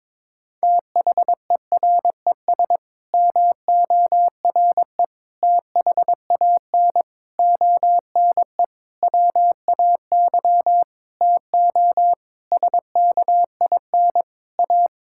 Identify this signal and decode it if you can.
Morse code, heard as a keyed tone, about 22 wpm, 700 Hz. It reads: THERES MORE THAN ONE WAY TO SKIN A
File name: Sets of 9 Words - 22wpm